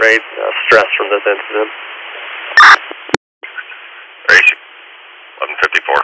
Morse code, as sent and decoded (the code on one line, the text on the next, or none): none